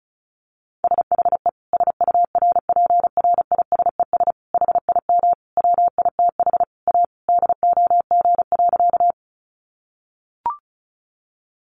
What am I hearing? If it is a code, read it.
Morse code, heard as a keyed tone, about 34 wpm, 700 Hz. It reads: SHE SURPRISES HIM WITH A DOG.
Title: Additional Easy Phrases 35wpm